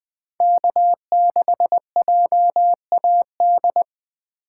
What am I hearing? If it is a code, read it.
Morse code, heard as a keyed tone, about 20 wpm, 700 Hz. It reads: K6JAD
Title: Speed Racing - USA 1x3 Call Signs 17wpm